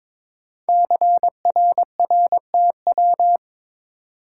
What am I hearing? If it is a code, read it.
Morse code, heard as a keyed tone, about 22 wpm, 700 Hz. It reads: CRRTW
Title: Code Groups - 5 Letters - 22wpm